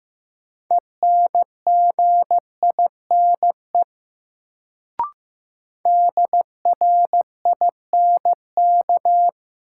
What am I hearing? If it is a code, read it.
Morse code, heard as a keyed tone, about 15 wpm, 700 Hz. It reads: ENGINE DRINK
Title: Sentences from Top 1500 Words - New Word Drill Alt - 15wpm